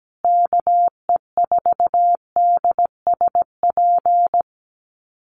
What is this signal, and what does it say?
Morse code, heard as a keyed tone, about 17 wpm, 700 Hz. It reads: KE4DSP
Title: USA Call Signs Repeated 3x 17wpm